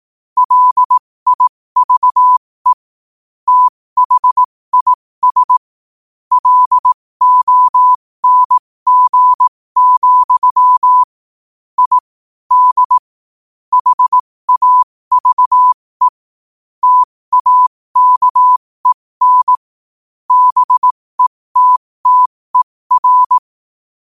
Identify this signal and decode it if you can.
Morse code, heard as a keyed tone, about 18 wpm, spaced as for 15 wpm, 1000 Hz. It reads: LIVE THIS LONG, I D HAVE TAKEN BETTER
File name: qotdfiles / 15wpm